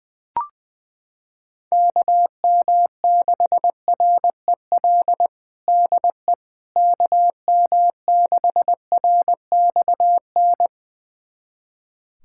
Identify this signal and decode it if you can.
Morse code, heard as a keyed tone, about 20 wpm, 700 Hz. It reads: KM6REL DE KM6RXN
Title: USA Callsign DE Callsign - 6 Characters - 20wpm